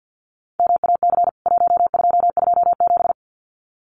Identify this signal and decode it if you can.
Morse code, heard as a keyed tone, about 48 wpm, 700 Hz. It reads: MAC 1227
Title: CWT Exchange 50wpm